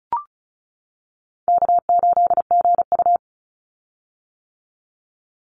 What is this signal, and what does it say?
Morse code, heard as a keyed tone, about 36 wpm, 700 Hz. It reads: K8GU
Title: Call Signs - 35wpm